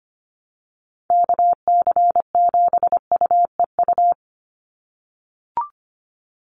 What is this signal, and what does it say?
Morse code, heard as a keyed tone, about 25 wpm, 700 Hz. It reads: KC7UEU
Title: USA Call Signs 25wpm